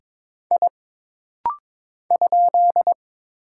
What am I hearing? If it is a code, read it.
Morse code, heard as a keyed tone, about 22 wpm, 700 Hz. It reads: I ?
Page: This rapid-fire timing is geared to push your ICR (Instant Character Recognition) skills.